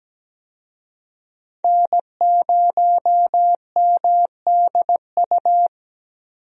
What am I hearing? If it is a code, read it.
Morse code, heard as a keyed tone, about 17 wpm, 700 Hz. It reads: N0MDU
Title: USA Call Signs - 5 Characters - Repeated 3x - 17wpm